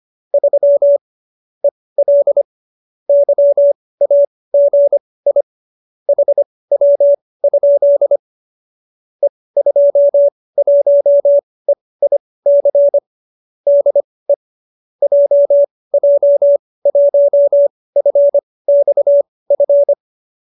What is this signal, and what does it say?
Morse code, heard as a keyed tone, about 25 wpm, spaced as for 18 wpm, 570 Hz. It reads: 3 EL YAGI HW? E21EIC DE JJ1FXF